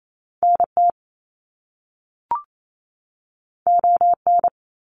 Morse code, heard as a keyed tone, about 28 wpm, 700 Hz. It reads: NT ON
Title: Canadian Provinces and Territories 28wpm